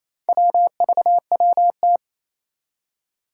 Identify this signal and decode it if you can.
Morse code, heard as a keyed tone, about 28 wpm, 700 Hz. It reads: WVWT
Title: Four Challenging Letter-Number Combinations 28wpm